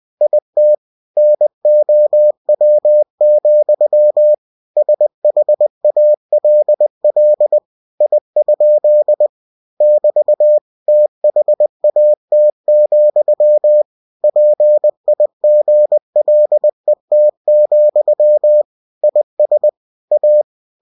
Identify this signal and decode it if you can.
Morse code, heard as a keyed tone, about 20 wpm, 600 Hz. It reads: IT NOW, SHALL I? <BT> THAT, PIGLET, IS A
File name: cw-books / winnie-the-pooh / winnie-20wpm-page-0049